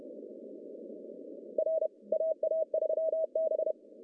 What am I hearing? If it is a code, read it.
Morse code, heard as a keyed tone, about 30 wpm, 630 Hz. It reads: R AA3B
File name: CW / cqww-cw-k5tr